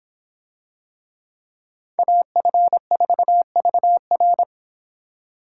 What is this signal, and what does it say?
Morse code, heard as a keyed tone, about 25 wpm, 700 Hz. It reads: AF4VR